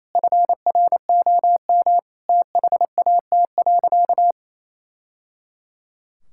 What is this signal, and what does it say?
Morse code, heard as a keyed tone, about 28 wpm, 700 Hz. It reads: FROM THAT.
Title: Sentences from Top 1000 Words - Review Part 5 - 28wpm